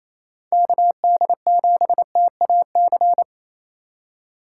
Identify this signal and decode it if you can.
Morse code, heard as a keyed tone, about 28 wpm, 700 Hz. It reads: KD7TAC
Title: USA Call Signs - 6 Characters - 2x Character Spacing - 28wpm